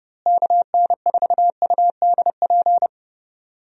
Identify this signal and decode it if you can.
Morse code, heard as a keyed tone, about 30 wpm, 700 Hz. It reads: KN4UDP
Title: USA Call Signs - 6 Characters - Repeated 3x - 30wpm